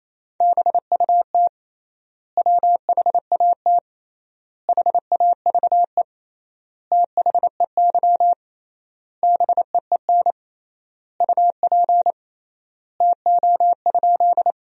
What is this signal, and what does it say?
Morse code, heard as a keyed tone, about 28 wpm, 700 Hz. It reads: BUT WHAT HAVE THEY BEEN UP TO?
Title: 3x Word-Spacing - Sentences from Top 100 Words - 28wpm